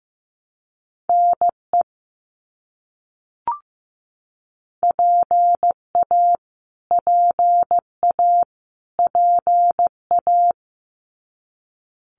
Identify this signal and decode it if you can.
Morse code, heard as a keyed tone, about 15 wpm, 700 Hz. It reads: NE PA PA PA
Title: US States Repeated 3x 15wpm